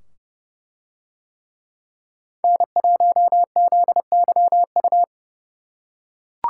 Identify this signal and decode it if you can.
Morse code, heard as a keyed tone, about 29 wpm, 700 Hz. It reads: N1ZYU